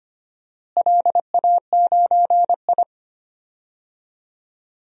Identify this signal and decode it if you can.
Morse code, heard as a keyed tone, about 25 wpm, 700 Hz. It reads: LA9I
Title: DXCC-100 Call Signs - 4 Characters - Repeated 3x - 25wpm